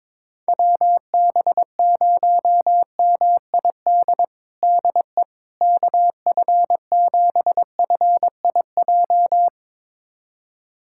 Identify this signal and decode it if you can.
Morse code, heard as a keyed tone, about 22 wpm, 700 Hz. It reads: WB0MID DE KF7FIJ
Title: USA Callsign DE Callsign - 6 Characters - 22wpm